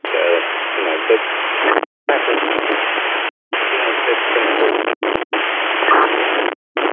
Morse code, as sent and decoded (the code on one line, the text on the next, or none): none